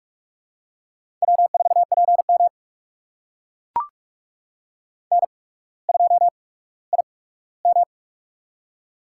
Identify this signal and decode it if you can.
Morse code, heard as a keyed tone, about 44 wpm, spaced as for 19 wpm, 690 Hz. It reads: W4PM N2IM
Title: USA Call Signs - 4 Characters - 8x Character Spacing - 45wpm